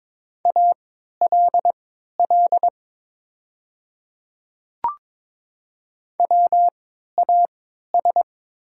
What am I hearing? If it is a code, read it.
Morse code, heard as a keyed tone, about 22 wpm, spaced as for 13 wpm, 700 Hz. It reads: ALL WAS
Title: 3x Character Spacing - Top 200 Words - 22wpm